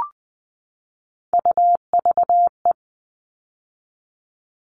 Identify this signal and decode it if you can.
Morse code, heard as a keyed tone, about 20 wpm, 700 Hz. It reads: UVE
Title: All Cut Numbers - 3-Digit 20wpm